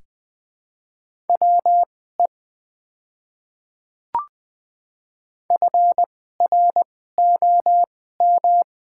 Morse code, heard as a keyed tone, about 20 wpm, spaced as for 14 wpm, 690 Hz. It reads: WE FROM
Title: 2x Character Spacing - Top 100 Words - 20wpm